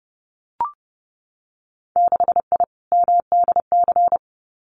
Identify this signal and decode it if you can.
Morse code, heard as a keyed tone, about 30 wpm, 700 Hz. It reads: 6I MDC
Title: Winter Field Day Exchanges Encore 30wpm